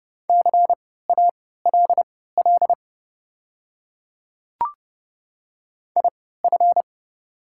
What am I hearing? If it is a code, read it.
Morse code, heard as a keyed tone, about 30 wpm, spaced as for 17 wpm, 700 Hz. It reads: CALL IF
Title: 3x Character Spacing - Top 100 Words - 30wpm